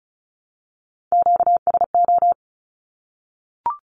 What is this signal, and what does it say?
Morse code, heard as a keyed tone, about 34 wpm, 700 Hz. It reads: QSO
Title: Q-Codes Repeated 3x 35wpm